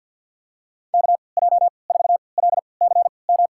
Morse code, heard as a keyed tone, about 51 wpm, spaced as for 28 wpm, 690 Hz. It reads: KJ4LCM
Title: USA Call Signs - 6 Characters - 3x Character Spacing - 50wpm